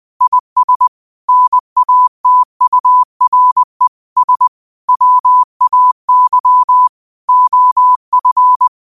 Morse code, heard as a keyed tone, about 20 wpm, 1000 Hz. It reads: IS NATURE S WAY OF